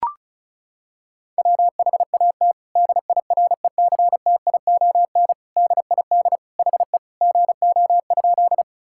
Morse code, heard as a keyed tone, about 36 wpm, 700 Hz. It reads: WHAT DIRECTION DID HE GO?
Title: Sentences from Top 1000 Words - Review Part 5 - 35wpm